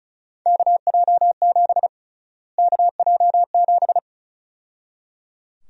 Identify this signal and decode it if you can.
Morse code, heard as a keyed tone, about 34 wpm, 700 Hz. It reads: KJ7 KJ7
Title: USA Call Sign Prefixes - Repeated 2x - 35wpm